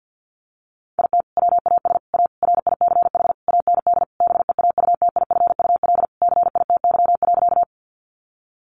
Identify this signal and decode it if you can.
Morse code, heard as a keyed tone, about 50 wpm, 700 Hz. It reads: IT WAS A RICH AND BEAUTIFUL CITY.
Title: Sentences from Top 1100 Words 50wpm